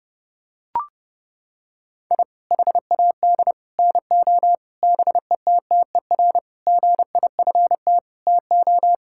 Morse code, heard as a keyed tone, about 28 wpm, 700 Hz. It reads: I HAD NO BETTER GIFT TO
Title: Sentences from Top 1000 Words - Review Part 6 - 30wpm